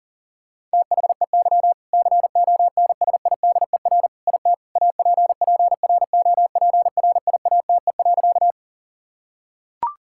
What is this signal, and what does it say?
Morse code, heard as a keyed tone, about 37 wpm, 700 Hz. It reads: THEY CONSIDER IT APPROPRIATE.
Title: Sentences from Top 1000 Words - Review Part 5 - 40wpm